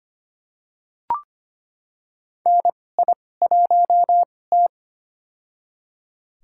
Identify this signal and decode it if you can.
Morse code, heard as a keyed tone, about 25 wpm, spaced as for 18 wpm, 700 Hz. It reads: NI1T
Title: USA Call Signs - 4 Characters - 2x Character Spacing - 25wpm